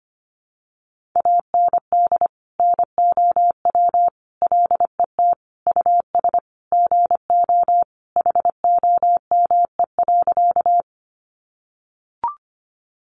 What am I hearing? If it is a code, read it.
Morse code, heard as a keyed tone, about 25 wpm, 700 Hz. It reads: AND NOW LET US GO HOME.
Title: Sentences from Top 300 Words 25wpm